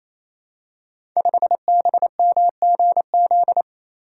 Morse code, heard as a keyed tone, about 28 wpm, 700 Hz. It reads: 5BMGZ
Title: Code Groups - 5 Letter:Numbers - 28wpm